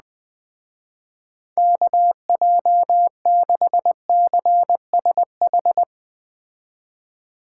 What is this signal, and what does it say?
Morse code, heard as a keyed tone, about 20 wpm, 700 Hz. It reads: KJ6CSH
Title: Speed Racing - USA 2x3 Call Signs 17wpm